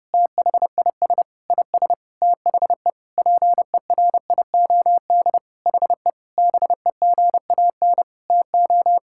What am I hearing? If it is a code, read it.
Morse code, heard as a keyed tone, about 30 wpm, 700 Hz. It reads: THIS IS THE PERIOD HE BEGAN TO
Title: Sentences from Top 1000 Words - Review Part 5 - 30wpm